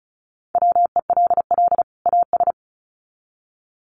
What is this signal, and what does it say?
Morse code, heard as a keyed tone, about 34 wpm, 700 Hz. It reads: WELL AS
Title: Sets of 2 Words 35wpm